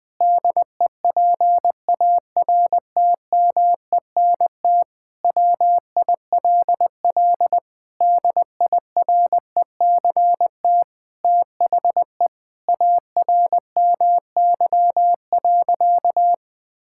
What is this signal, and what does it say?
Morse code, heard as a keyed tone, about 20 wpm, 700 Hz. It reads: DEPARTMENT WILL DIRECT THE ARMY.
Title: Sentences from Top 1000 Words - Review Part 5 - 20wpm